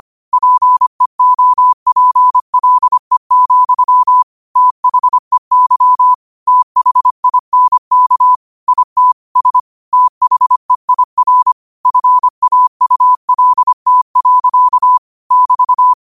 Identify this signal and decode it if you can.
Morse code, heard as a keyed tone, about 25 wpm, 1000 Hz. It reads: PEOPLE, THEY THINK IT S THEIR FAULT. <BT>
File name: 25wpm